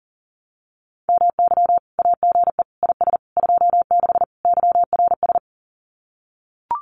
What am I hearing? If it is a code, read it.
Morse code, heard as a keyed tone, about 40 wpm, 700 Hz. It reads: MY AGE IS 26 YRS
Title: Common QSO Phrases Encore 40wpm